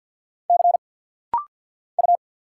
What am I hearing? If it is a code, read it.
Morse code, heard as a keyed tone, about 55 wpm, 700 Hz. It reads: C U
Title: Single Letter-Number Alt - Rapid-Fire 50wpm